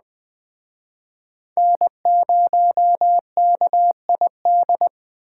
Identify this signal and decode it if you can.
Morse code, heard as a keyed tone, about 20 wpm, 700 Hz. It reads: N0KID